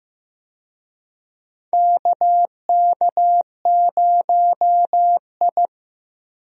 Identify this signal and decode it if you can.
Morse code, heard as a keyed tone, about 15 wpm, 700 Hz. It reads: KK0I
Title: USA Call Signs - 4 Characters - Repeated 3x - 15wpm